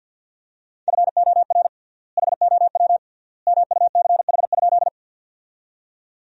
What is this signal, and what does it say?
Morse code, heard as a keyed tone, about 50 wpm, 700 Hz. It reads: FOR HOW MUCH?
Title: 3x Word-Spacing - Sentences from Top 200 Words - 50wpm